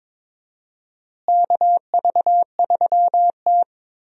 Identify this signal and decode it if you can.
Morse code, heard as a keyed tone, about 22 wpm, 700 Hz. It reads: KV3T
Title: USA Call Signs - 4 Characters - Repeated 2x - 22wpm